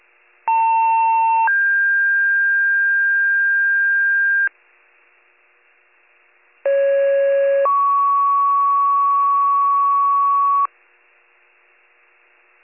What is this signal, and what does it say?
Two-tone page, heard as a one-second tone, then a three-second tone, about 900 then 1600 Hz, about 550 then 1120 Hz. It is heard from 0.5 to 10.7 s.